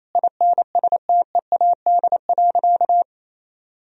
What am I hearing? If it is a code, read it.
Morse code, heard as a keyed tone, about 28 wpm, 700 Hz. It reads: INSTEAD.
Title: Sentences from Top 1000 Words - Review Part 4 - 28wpm